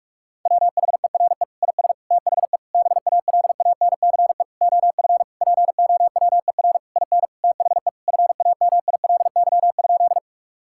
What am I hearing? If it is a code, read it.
Morse code, heard as a keyed tone, about 46 wpm, 700 Hz. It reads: WHERE IS THE BALANCE OF POWER IN THE FAMILY?
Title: Sentences from Top 1000 Words - Review Part 6 - 45wpm